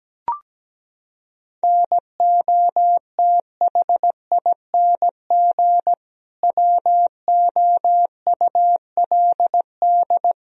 Morse code, heard as a keyed tone, about 17 wpm, 700 Hz. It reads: NOTHING WOULD
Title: Sentences from Top 1000 Words - Review Part 3 - 17wpm